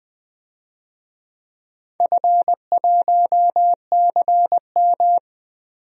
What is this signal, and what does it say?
Morse code, heard as a keyed tone, about 20 wpm, 700 Hz. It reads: F1CM
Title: DXCC-100 Call Signs - 4 Characters - Repeated 2x - 20wpm